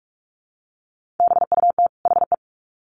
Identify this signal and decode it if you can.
Morse code, heard as a keyed tone, about 44 wpm, 700 Hz. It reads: BUT HE
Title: Sets of 2 Words 45wpm